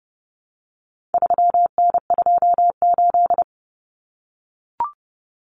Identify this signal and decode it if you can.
Morse code, heard as a keyed tone, about 30 wpm, 700 Hz. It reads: 3N28
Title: Common Cut Numbers - 4-Digit 30wpm